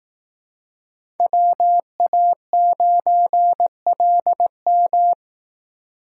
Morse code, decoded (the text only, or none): WA9LM